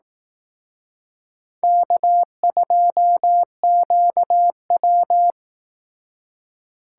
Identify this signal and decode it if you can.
Morse code, heard as a keyed tone, about 18 wpm, 700 Hz. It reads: K2QW
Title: Speed Racing - USA 1x2 Call Signs 15wpm